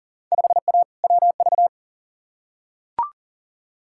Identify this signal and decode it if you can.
Morse code, heard as a keyed tone, about 39 wpm, 700 Hz. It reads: 5A WV
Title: Field Day Exchanges - 40wpm